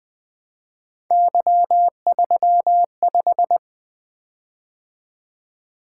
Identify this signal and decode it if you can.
Morse code, heard as a keyed tone, about 20 wpm, 700 Hz. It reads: Y35
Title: Three Letter-Number Combinations - 20wpm